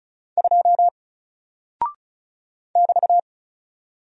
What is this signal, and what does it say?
Morse code, heard as a keyed tone, about 36 wpm, 700 Hz. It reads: 2 <BT>
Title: Single Character (Letter:Number:Punctuation:Prosign) Alt 35wpm